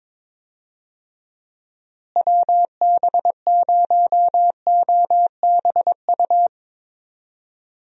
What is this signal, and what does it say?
Morse code, heard as a keyed tone, about 22 wpm, 700 Hz. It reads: WB0OBU
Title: USA Call Signs - 6 Characters - Repeated 3x - 22wpm